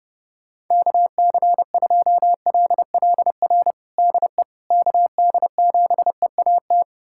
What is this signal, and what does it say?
Morse code, heard as a keyed tone, about 30 wpm, 700 Hz. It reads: KC2LLR DE KD7EAT
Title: USA Callsign DE Callsign - 6 Characters - 30wpm